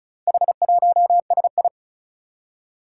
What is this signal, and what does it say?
Morse code, heard as a keyed tone, about 36 wpm, 700 Hz. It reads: H1SI
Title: Four Challenging Letter-Number Combinations 35wpm